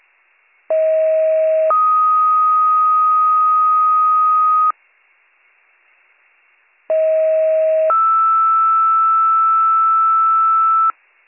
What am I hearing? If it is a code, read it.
Two-tone page, heard as a one-second tone, then a three-second tone, about 640 then 1250 Hz, about 640 then 1320 Hz. It is heard from 0.7 to 10.9 s.